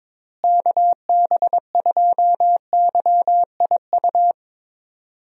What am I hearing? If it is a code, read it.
Morse code, heard as a keyed tone, about 22 wpm, 700 Hz. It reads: KB2YIU
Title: USA Call Signs - 6 Characters - Repeated 3x - 22wpm